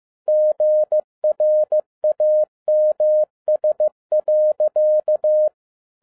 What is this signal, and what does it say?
Morse code, heard as a keyed tone, about 15 wpm, 610 Hz. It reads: GRAMS.
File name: 2012-01-mp3 / n07-ARA JR0DIL-15wpm